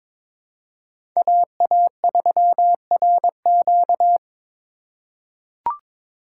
Morse code, heard as a keyed tone, about 22 wpm, 700 Hz. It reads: AA3RQ
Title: USA Call Signs - 5 Characters - 4x Character Spacing - 22wpm